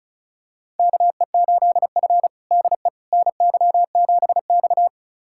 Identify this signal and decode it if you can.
Morse code, heard as a keyed tone, about 36 wpm, 700 Hz. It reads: KE8F DE NY7X
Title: USA Callsign DE Callsign - 4 Characters - 35wpm